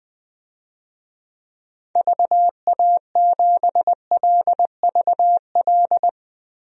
Morse code, heard as a keyed tone, about 20 wpm, 700 Hz. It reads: VA7LVL
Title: DXCC-100 Call Signs - 6 Characters - 20wpm